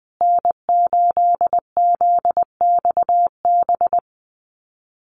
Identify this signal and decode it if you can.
Morse code, heard as a keyed tone, about 20 wpm, 700 Hz. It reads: N8ZXB
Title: USA Call Signs Repeated 3x 20wpm